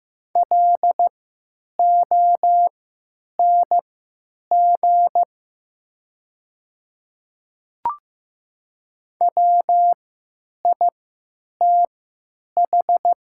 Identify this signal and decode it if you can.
Morse code, heard as a keyed tone, about 15 wpm, spaced as for 9 wpm, 690 Hz. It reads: LONG WITH
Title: 3x Character Spacing - Top 100 Words - 15wpm